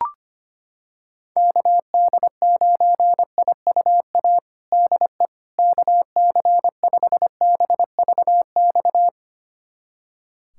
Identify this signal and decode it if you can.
Morse code, heard as a keyed tone, about 25 wpm, 700 Hz. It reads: KD9IUA DE KC5BVX
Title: USA Callsign DE Callsign 25wpm